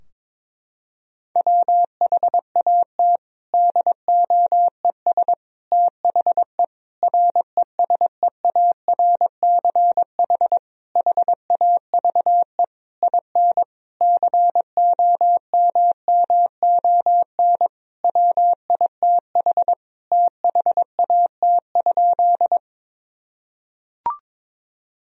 Morse code, heard as a keyed tone, about 22 wpm, 700 Hz. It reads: WHAT DOES THE RESEARCH HAVE IN COMMON WITH THAT?
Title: Sentences from Top 1000 Words - Review Part 4 - 22wpm